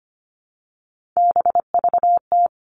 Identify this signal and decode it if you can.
Morse code, heard as a keyed tone, about 25 wpm, 700 Hz. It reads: BVT
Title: All Cut Numbers - 3-Digit 25wpm